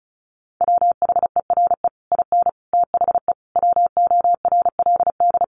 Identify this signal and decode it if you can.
Morse code, heard as a keyed tone, about 34 wpm, 700 Hz. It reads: WHERE IN THE WORLD
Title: Sentences from Top 200 Words 35wpm